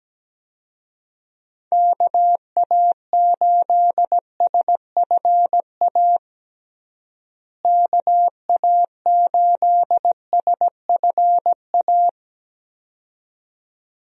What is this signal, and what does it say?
Morse code, heard as a keyed tone, about 17 wpm, 700 Hz. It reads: KA8SFA KA8SFA
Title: USA Call Signs - 6 Characters - Repeated 2x - 17wpm